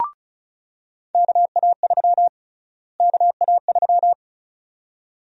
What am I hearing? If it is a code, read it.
Morse code, heard as a keyed tone, about 34 wpm, 700 Hz. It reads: KA3 KA3
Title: USA Call Sign Prefixes - Repeated 2x - 35wpm